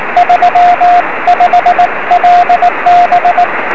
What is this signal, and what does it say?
Morse code, heard as a keyed tone, about 19 wpm, 710 Hz, with radio noise. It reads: 35LB
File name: 0559z ew6fs 23cm